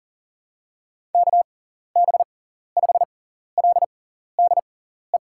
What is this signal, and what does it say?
Morse code, heard as a keyed tone, about 37 wpm, spaced as for 14 wpm, 700 Hz. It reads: KB5LDE
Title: USA Call Signs - 6 Characters - 6x Character Spacing - 40wpm